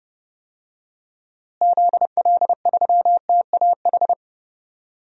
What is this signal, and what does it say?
Morse code, heard as a keyed tone, about 31 wpm, 700 Hz. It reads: ZL3TAH
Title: DXCC-100 Call Signs - 6 Characters - Repeated 3x - 30wpm